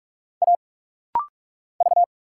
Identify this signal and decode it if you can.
Morse code, heard as a keyed tone, about 45 wpm, 700 Hz. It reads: A V
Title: Single Letters - Rapid-Fire 45wpm